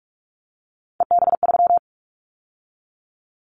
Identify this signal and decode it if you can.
Morse code, heard as a keyed tone, about 46 wpm, 700 Hz. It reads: EB3
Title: Callsign Prefixes for Top 25 DXCC - 45wpm